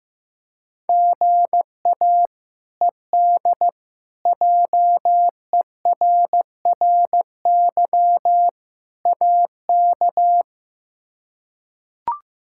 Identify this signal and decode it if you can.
Morse code, heard as a keyed tone, about 15 wpm, 700 Hz. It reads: GA ED JERRY AK
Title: K1USN SST QSO Practice - 15wpm